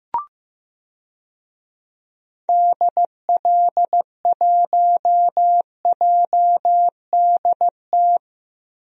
Morse code, heard as a keyed tone, about 15 wpm, 700 Hz. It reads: DL1JDT
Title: DXCC-100 Call Signs - 6 Characters - Repeated 3x - 15wpm